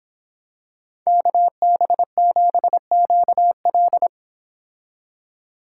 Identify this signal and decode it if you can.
Morse code, heard as a keyed tone, about 26 wpm, 700 Hz. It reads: KB7QL